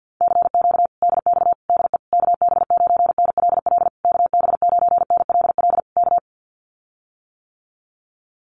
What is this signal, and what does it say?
Morse code, heard as a keyed tone, about 50 wpm, 700 Hz. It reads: CQ DX DE KB9NLL KB9NLL K
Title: Common QSO Phrases Encore 50wpm